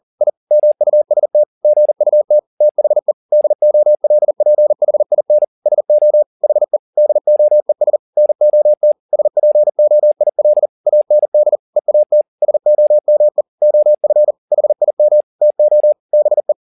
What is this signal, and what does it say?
Morse code, heard as a keyed tone, about 40 wpm, 600 Hz. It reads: I MUST GUT THE DOLPHIN SO HE DOES NOT SPOIL AND EAT SOME OF HIM TO BE
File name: the old man and the sea 2 40wpm